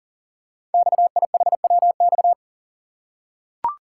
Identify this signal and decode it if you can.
Morse code, heard as a keyed tone, about 39 wpm, 690 Hz. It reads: XIHWX
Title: Code Groups - 5 Letters - 40wpm